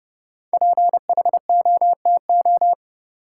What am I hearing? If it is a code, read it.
Morse code, heard as a keyed tone, about 31 wpm, 700 Hz. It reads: PHOTO
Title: Sentences from Top 1200 Words - New Word Drill - 30wpm